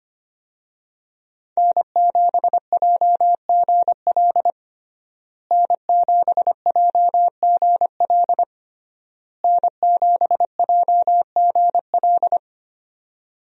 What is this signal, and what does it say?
Morse code, heard as a keyed tone, about 25 wpm, 700 Hz. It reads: N7JGL N7JGL N7JGL
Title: USA Call Signs - 5 Characters - Repeated 3x - 25wpm